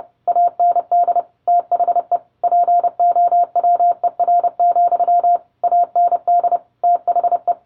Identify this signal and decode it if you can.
Morse code, heard as a keyed tone, about 30 wpm, 700 Hz. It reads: AND THE POWER, AND THE